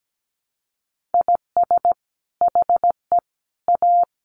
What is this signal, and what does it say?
Morse code, heard as a keyed tone, about 17 wpm, 700 Hz. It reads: IS HE A
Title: Sentences from Top 1500 Words 17wpm